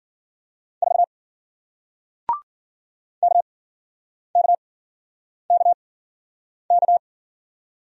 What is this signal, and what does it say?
Morse code, heard as a keyed tone, about 54 wpm, spaced as for 9 wpm, 700 Hz. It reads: 4 KKKK